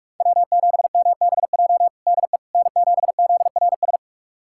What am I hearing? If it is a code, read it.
Morse code, heard as a keyed tone, about 42 wpm, 700 Hz. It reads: W7MBJ DE N7ZRS
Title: USA Callsign DE Callsign - 5 Characters - 45wpm